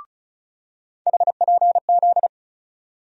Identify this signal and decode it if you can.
Morse code, heard as a keyed tone, about 34 wpm, 700 Hz. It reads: HPZ
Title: USA Call Sign Postfixes - 3 Letters 35wpm